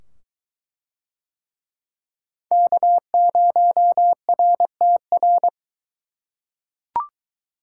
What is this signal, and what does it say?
Morse code, heard as a keyed tone, about 22 wpm, 700 Hz. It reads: K0RTR